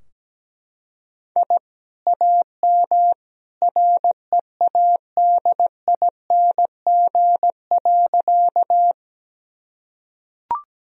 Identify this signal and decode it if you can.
Morse code, heard as a keyed tone, about 17 wpm, 700 Hz. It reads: I AM READING.
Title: Sentences from Top 1000 Words - Review Part 3 - 17wpm